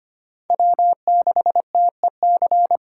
Morse code, heard as a keyed tone, about 25 wpm, 700 Hz. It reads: W6TEC
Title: USA Call Signs - 5 Characters - 2x Character Spacing - 25wpm